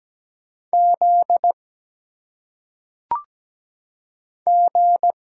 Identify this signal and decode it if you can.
Morse code, heard as a keyed tone, about 17 wpm, 700 Hz. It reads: Z G
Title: Single Letters - Phonetic Pronunciation 17wpm